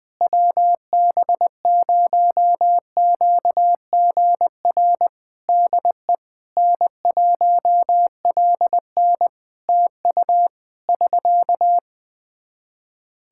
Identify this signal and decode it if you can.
Morse code, heard as a keyed tone, about 20 wpm, 700 Hz. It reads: WB0QGR DE N1LN TU <SK>
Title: Practice QSO - Exchange 4 - Ending - 20wpm